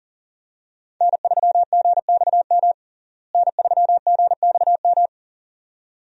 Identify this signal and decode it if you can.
Morse code, heard as a keyed tone, about 40 wpm, 700 Hz. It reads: N3GXM N3GXM
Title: USA Call Signs - 5 Characters - Repeated 2x - 40wpm